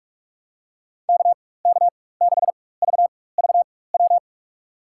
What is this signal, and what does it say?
Morse code, heard as a keyed tone, about 45 wpm, spaced as for 21 wpm, 700 Hz. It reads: KK6VVW
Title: USA Call Signs - 6 Characters - 4x Character Spacing - 45wpm